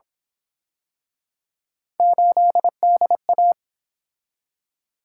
Morse code, heard as a keyed tone, about 28 wpm, 700 Hz. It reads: 8DA